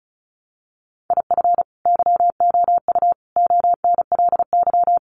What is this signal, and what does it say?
Morse code, heard as a keyed tone, about 34 wpm, 700 Hz. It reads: IF YOU ONLY
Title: Sentences from Top 400 Words 35wpm